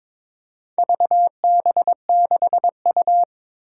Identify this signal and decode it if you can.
Morse code, heard as a keyed tone, about 22 wpm, 700 Hz. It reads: VB6U
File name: practice / 22wpm / Four Letter-Number Combinations - 22wpm